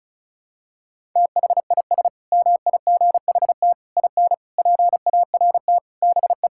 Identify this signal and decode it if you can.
Morse code, heard as a keyed tone, about 34 wpm, 700 Hz. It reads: THIS MIGHT IN PART BE
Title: Sentences from Top 1000 Words - Review Part 5 - 35wpm